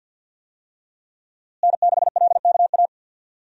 Morse code, heard as a keyed tone, about 50 wpm, 700 Hz. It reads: N6LKA
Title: USA Call Signs - 5 Characters - Repeated 3x - 50wpm